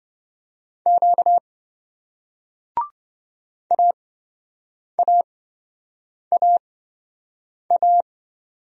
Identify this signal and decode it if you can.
Morse code, heard as a keyed tone, about 28 wpm, spaced as for 7 wpm, 700 Hz. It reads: Q AAAA